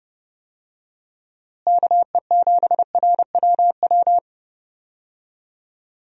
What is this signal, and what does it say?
Morse code, heard as a keyed tone, about 32 wpm, 700 Hz. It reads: KE7RWW
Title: USA Call Signs - 6 Characters - Repeated 2x - 30wpm